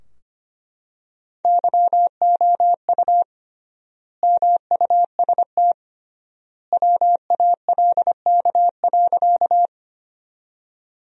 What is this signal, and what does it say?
Morse code, heard as a keyed tone, about 25 wpm, 700 Hz. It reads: YOU MUST WALK.
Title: 3x Word-Spacing - Sentences from Top 300 Words - 25wpm